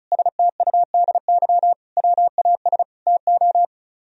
Morse code, heard as a keyed tone, about 43 wpm, 700 Hz. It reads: STUDY WAS TO
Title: Speed-Racing - Sets of 5 Words 35wpm